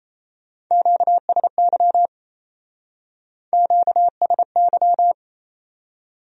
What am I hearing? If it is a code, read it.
Morse code, heard as a keyed tone, about 38 wpm, 700 Hz. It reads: QSY QSY